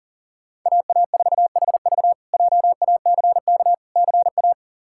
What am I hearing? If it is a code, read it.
Morse code, heard as a keyed tone, about 40 wpm, 700 Hz. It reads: AA4HV JACK CA
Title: K1USN SST QSO Practice - 40wpm